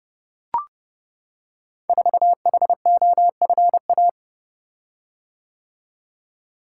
Morse code, heard as a keyed tone, about 29 wpm, 700 Hz. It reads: 4HOFA
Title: Code Groups - 5 Letter:Numbers - 30wpm